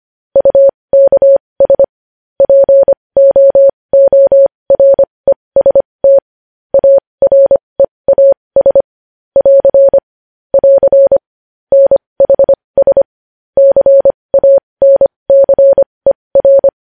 Morse code, heard as a keyed tone, about 25 wpm, spaced as for 20 wpm, 550 Hz. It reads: UKS POOREST AREAS <AR> <AR> NHS CANCER
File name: podcast / News headlines-1x-25x20